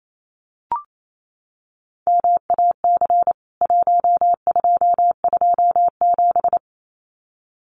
Morse code, heard as a keyed tone, about 28 wpm, 700 Hz. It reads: MAC 1227
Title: CWT Exchange 28wpm